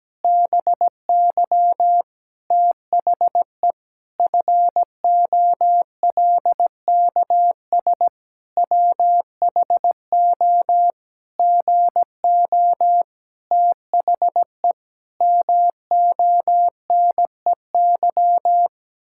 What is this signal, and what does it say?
Morse code, heard as a keyed tone, about 17 wpm, 700 Hz. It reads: BY THE FOLKS WHO GO THE MONEY
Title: Pangram Challenge v2 - 17wpm